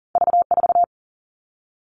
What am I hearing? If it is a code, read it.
Morse code, heard as a keyed tone, about 41 wpm, 700 Hz. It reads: V4
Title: All Cut Numbers - 2-Digit 40wpm